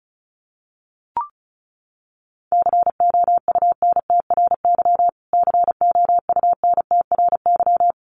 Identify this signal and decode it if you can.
Morse code, heard as a keyed tone, about 34 wpm, 700 Hz. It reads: COUNTRY COUNTRY
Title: Top 500 Words Repeated 35wpm